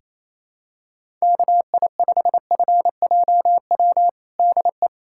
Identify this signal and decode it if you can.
Morse code, heard as a keyed tone, about 28 wpm, 700 Hz. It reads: KI5FJW DE
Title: USA Callsign DE Callsign - 6 Characters - 28wpm